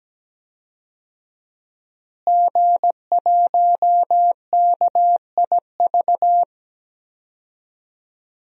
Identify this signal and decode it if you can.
Morse code, heard as a keyed tone, about 17 wpm, 700 Hz. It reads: G1KIV
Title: DXCC-100 Call Signs - 5 Characters - Repeated 2x - 17wpm